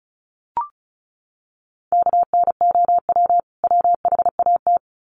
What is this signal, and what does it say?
Morse code, heard as a keyed tone, about 34 wpm, 700 Hz. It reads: KNOW WHAT
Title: Sets of 2 Words 35wpm